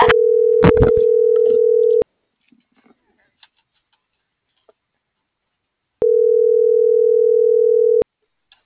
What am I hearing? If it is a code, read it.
Ringback tone (North America), 440 + 480 Hz.